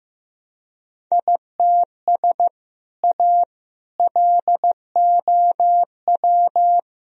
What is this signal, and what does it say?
Morse code, heard as a keyed tone, about 15 wpm, 700 Hz. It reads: ITS A LOW
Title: Sentences from Top 1000 Words - Review Part 1 - 15wpm